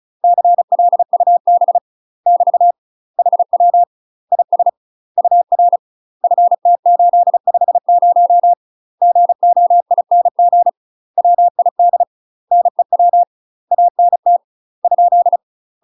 Morse code, heard as a keyed tone, about 36 wpm, 700 Hz. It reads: CLUB <BT> HW IS UR FT850 GOING WID NEW ANT ?
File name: UK MASTER QSOs TRAINING / 35WPM / UK MASTER QSOs 0032